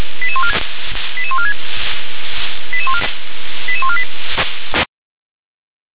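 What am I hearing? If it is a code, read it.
Five-tone selective call (ZVEI), heard as quick runs of five tones, tones of about 70 ms. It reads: 88146 88148
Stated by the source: ZVEI five-tone selective call received by SDR.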